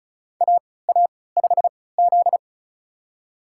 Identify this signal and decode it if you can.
Morse code, heard as a keyed tone, about 34 wpm, spaced as for 20 wpm, 700 Hz. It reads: AA5Z
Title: USA Call Signs - 4 Characters - 3x Character Spacing - 35wpm